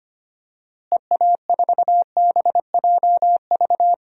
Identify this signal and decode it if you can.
Morse code, heard as a keyed tone, about 25 wpm, 700 Hz. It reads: EA4BJV
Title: DXCC-100 Call Signs - 6 Characters - Repeated 3x - 25wpm